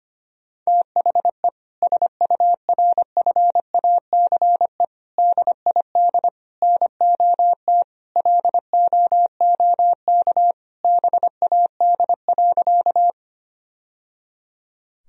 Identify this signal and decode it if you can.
Morse code, heard as a keyed tone, about 25 wpm, 700 Hz. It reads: THE SURFACE DID NOT LOOK BAD.
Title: Sentences from Top 1000 Words - Review Part 3 - 25wpm